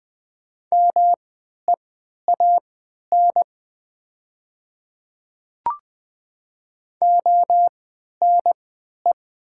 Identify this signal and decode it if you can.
Morse code, heard as a keyed tone, about 20 wpm, spaced as for 11 wpm, 700 Hz. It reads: MEAN ONE
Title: 3x Character Spacing - Top 500 Words - 20wpm